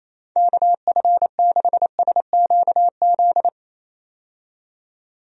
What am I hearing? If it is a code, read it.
Morse code, heard as a keyed tone, about 28 wpm, 700 Hz. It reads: KF6SQZ
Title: USA Call Signs - 6 Characters - 28wpm